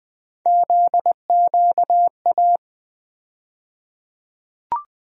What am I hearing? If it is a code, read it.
Morse code, heard as a keyed tone, about 20 wpm, 700 Hz. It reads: ZQA
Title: USA Call Sign Postfixes - 3 Letters 20wpm